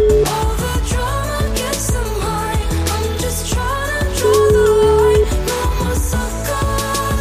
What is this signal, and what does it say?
Ringback tone (Europe, 425 Hz), with noise.